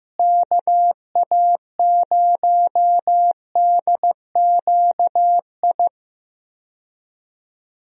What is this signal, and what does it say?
Morse code, heard as a keyed tone, about 16 wpm, 700 Hz. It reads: KA0DQI
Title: USA Call Signs - 6 Characters - Repeated 2x - 15wpm